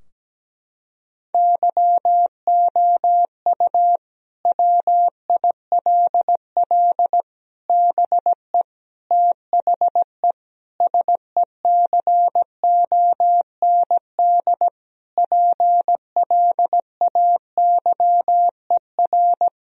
Morse code, heard as a keyed tone, about 17 wpm, 700 Hz. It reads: YOU WILL BE THE SECOND PLAYER
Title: Sentences from Top 1000 Words - Review Part 4 - 17wpm